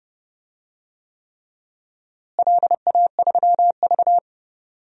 Morse code, heard as a keyed tone, about 32 wpm, 700 Hz. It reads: LA3V
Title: DXCC-100 Call Signs - 4 Characters - Repeated 3x - 30wpm